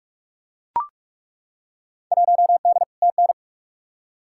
Morse code, heard as a keyed tone, about 41 wpm, 700 Hz. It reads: 1D TN
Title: Field Day Exchanges - 45wpm